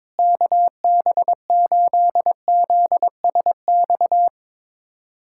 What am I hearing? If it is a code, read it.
Morse code, heard as a keyed tone, about 22 wpm, 700 Hz. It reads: KB8ZSX
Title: USA Callsign DE Callsign - 6 Characters - 22wpm